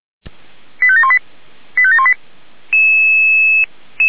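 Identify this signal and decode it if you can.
Five-tone selective call (ZVEI), heard as quick runs of five tones, tones of about 70 ms. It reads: 85718 85718
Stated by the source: Five-tone ZVEI selective call, code 89386, received by SDR.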